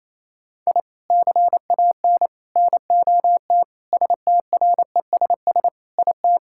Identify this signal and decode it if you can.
Morse code, heard as a keyed tone, about 28 wpm, 700 Hz. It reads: I CAN NOT STRESS IT
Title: Sentences from Top 1000 Words - Review Part 6 - 28wpm